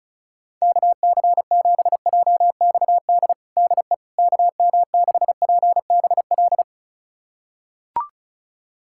Morse code, heard as a keyed tone, about 36 wpm, 700 Hz. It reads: KC7JXD DE KM6PBL
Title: USA Callsign DE Callsign - 6 Characters - 35wpm